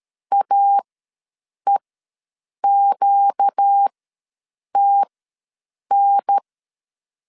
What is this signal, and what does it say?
Morse code, heard as a keyed tone, about 13 wpm, spaced as for 7 wpm, 800 Hz. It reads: AEQTN
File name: LEZIONI / 6 PKX0 / 6 PK ws 12wpm 3116